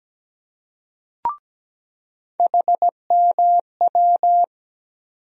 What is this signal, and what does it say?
Morse code, heard as a keyed tone, about 17 wpm, 700 Hz. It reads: HMW
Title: USA Call Sign Postfixes - 3 Letters 17wpm